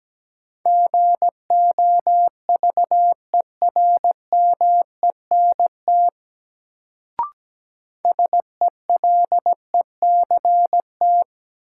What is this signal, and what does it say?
Morse code, heard as a keyed tone, about 17 wpm, 700 Hz. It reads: GOVERMENT SELECT
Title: Sentences from Top 1300 Words - New Word Drill Alt - 17wpm